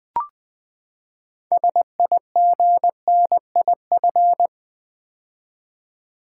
Morse code, heard as a keyed tone, about 20 wpm, 700 Hz. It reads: SIGNIF
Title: Build A Word - 6 Letters At A Time - 12-Letter Words - 20wpm